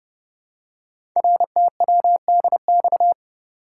Morse code, heard as a keyed tone, about 30 wpm, 700 Hz. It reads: RTWDX
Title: Code Groups - 5 Letters - 30wpm